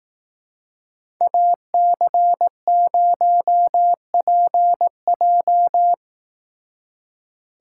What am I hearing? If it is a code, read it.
Morse code, heard as a keyed tone, about 18 wpm, 700 Hz. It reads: AC0PJ
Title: Speed Racing - USA Call Signs 15wpm